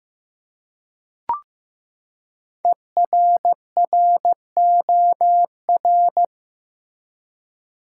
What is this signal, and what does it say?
Morse code, heard as a keyed tone, about 15 wpm, 700 Hz. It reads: ERROR
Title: Five Letter Words - 15wpm